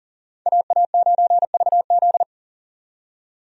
Morse code, heard as a keyed tone, about 40 wpm, 700 Hz. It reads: AA9VZ
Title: USA Call Signs - 5 Characters - 40wpm